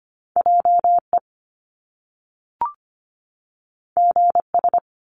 Morse code, heard as a keyed tone, about 25 wpm, 700 Hz. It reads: JE GS
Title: Top Two Letter Combinations in Words 25wpm